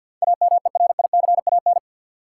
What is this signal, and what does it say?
Morse code, heard as a keyed tone, about 53 wpm, 700 Hz. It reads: AMERICAN
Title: Sets of 2 Words - New Word Drill - 50wpm